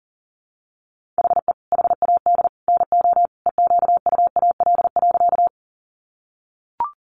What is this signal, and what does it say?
Morse code, heard as a keyed tone, about 37 wpm, 700 Hz. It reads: HE HAD NO EQUAL.
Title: Sentences from Top 1400 Words 40wpm